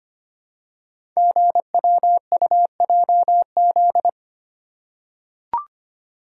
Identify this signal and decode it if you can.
Morse code, heard as a keyed tone, about 25 wpm, 690 Hz. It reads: GWUJZ
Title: Code Groups - 5 Letters - 25wpm